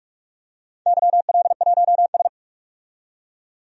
Morse code, heard as a keyed tone, about 42 wpm, 700 Hz. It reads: YL1S
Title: DXCC-100 Call Signs - 4 Characters - 45wpm